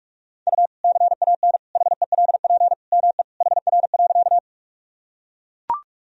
Morse code, heard as a keyed tone, about 42 wpm, 700 Hz. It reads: U CAN HELP ME HR.
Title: Sentences from Top 200 Words and Abbreviations 45wpm